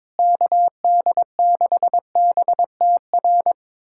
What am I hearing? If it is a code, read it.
Morse code, heard as a keyed tone, about 22 wpm, 700 Hz. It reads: KD6BTR
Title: USA Callsign DE Callsign 22wpm